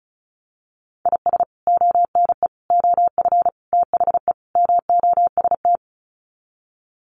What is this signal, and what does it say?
Morse code, heard as a keyed tone, about 34 wpm, 700 Hz. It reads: IS ONE OF THE MOST
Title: Sets of 5 Words 35wpm